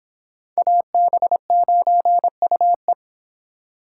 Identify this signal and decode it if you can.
Morse code, heard as a keyed tone, about 25 wpm, 700 Hz. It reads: AB9UE